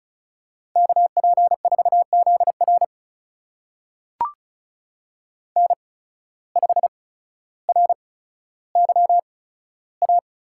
Morse code, heard as a keyed tone, about 36 wpm, spaced as for 15 wpm, 700 Hz. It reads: KP4ZR N5RYA
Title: USA Call Signs - 8x Character Spacing - 35wpm